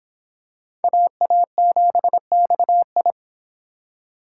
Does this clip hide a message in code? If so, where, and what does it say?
Yes: Morse code from 0.8 to 3.1 s, heard as a keyed tone, about 24 wpm, 700 Hz. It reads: AA7XI